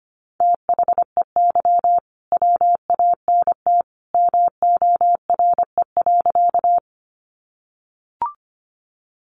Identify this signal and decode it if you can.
Morse code, heard as a keyed tone, about 25 wpm, 700 Hz. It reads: THEY WANT MORE.
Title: Sentences from Top 200 Words 25wpm